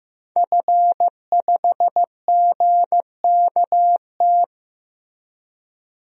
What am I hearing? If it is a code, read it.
Morse code, heard as a keyed tone, about 15 wpm, 700 Hz. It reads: F5GKT
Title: DXCC-100 Call Signs - 5 Characters - Repeated 3x - 15wpm